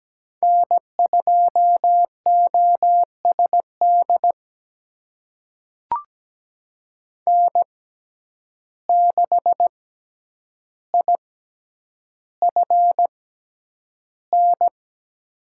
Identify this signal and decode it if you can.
Morse code, heard as a keyed tone, about 17 wpm, spaced as for 9 wpm, 700 Hz. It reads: N2OSD N6IFN
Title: USA Call Signs - 6x Character Spacing - 17wpm